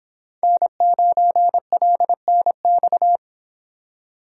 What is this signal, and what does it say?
Morse code, heard as a keyed tone, about 25 wpm, 700 Hz. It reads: N9LNX